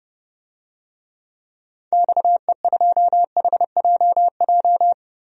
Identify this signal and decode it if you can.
Morse code, heard as a keyed tone, about 30 wpm, 700 Hz. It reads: XE2HJJ
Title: DXCC-100 Call Signs - 6 Characters - Repeated 3x - 30wpm